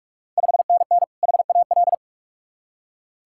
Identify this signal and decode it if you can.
Morse code, heard as a keyed tone, about 44 wpm, 700 Hz. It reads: 5NN HAL
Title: Ontario QSO Party Exchange 45wpm